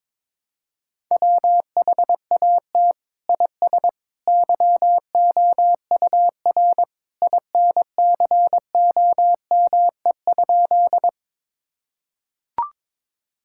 Morse code, heard as a keyed tone, about 22 wpm, 700 Hz. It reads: WHAT IS YOUR INCOME?
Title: Sentences from Top 1000 Words - Review Part 5 - 22wpm